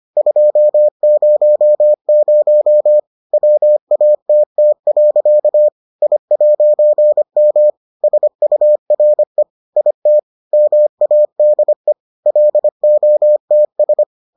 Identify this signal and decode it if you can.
Morse code, heard as a keyed tone, about 25 wpm, 600 Hz. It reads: 200 WATT. I'M SURE IT MADE LOTS